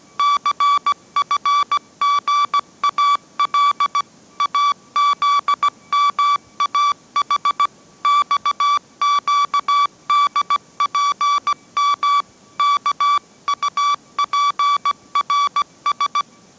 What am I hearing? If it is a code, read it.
Morse code, heard as a keyed tone, about 17 wpm, 1150 Hz. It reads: CFGAL AZMAH XQDPM KUPRS
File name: mrx / mrx080 18wpm